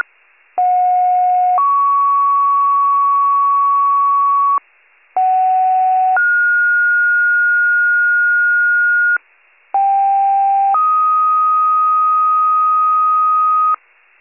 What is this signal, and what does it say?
Two-tone page, heard as a one-second tone, then a three-second tone, about 730 then 1120 Hz, about 750 then 1430 Hz, about 790 then 1250 Hz. It is heard from 0.6 to 13.8 s.